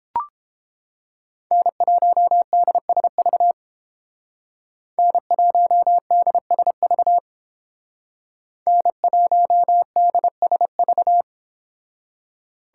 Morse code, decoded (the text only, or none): N1DSV N1DSV N1DSV